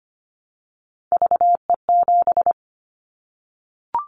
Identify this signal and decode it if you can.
Morse code, heard as a keyed tone, about 25 wpm, 700 Hz. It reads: VE7
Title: Callsign Prefixes for Top 100 DXCC - 25wpm